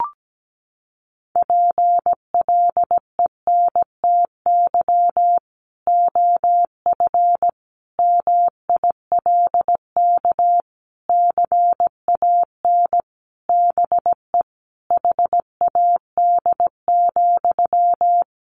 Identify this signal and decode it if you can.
Morse code, heard as a keyed tone, about 17 wpm, 700 Hz. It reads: PLENTY OF MILK CAN BE HAD,
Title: Sentences from Top 1500 Words 17wpm